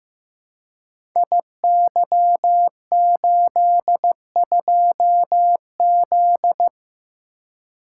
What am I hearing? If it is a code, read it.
Morse code, heard as a keyed tone, about 15 wpm, 700 Hz. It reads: IY82Z
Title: Code Groups - 5 Letter:Numbers - 15wpm